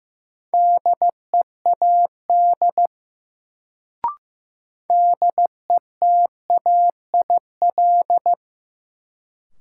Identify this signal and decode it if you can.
Morse code, heard as a keyed tone, about 15 wpm, 700 Hz. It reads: DEAD DETAIL
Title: Sentences from Top 1500 Words - New Word Drill - 15wpm